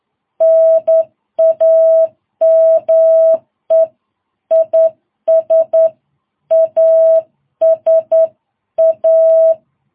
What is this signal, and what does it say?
Morse code, heard as a keyed tone, about 11 wpm, 660 Hz. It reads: NAME IS ASA
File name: jh8cbh / 2022cw